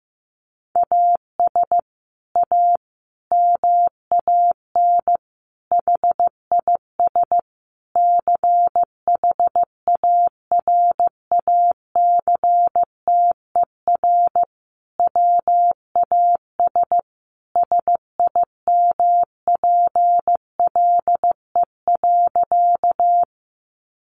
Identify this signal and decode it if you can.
Morse code, heard as a keyed tone, about 15 wpm, 700 Hz. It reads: AS A MAN HIS CHARACTER WAS SIMPLE.
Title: Sentences from Top 1200 Words 15wpm